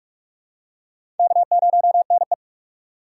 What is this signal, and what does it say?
Morse code, heard as a keyed tone, about 44 wpm, 700 Hz. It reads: K0NE
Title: USA Call Signs - 2x Character Spacing - 45wpm